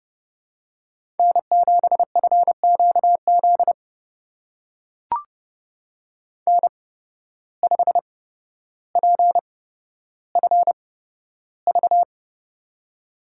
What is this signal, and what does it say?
Morse code, heard as a keyed tone, about 30 wpm, spaced as for 13 wpm, 700 Hz. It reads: N7FQZ N5PFV
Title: USA Call Signs - 5 Characters - 8x Character Spacing - 30wpm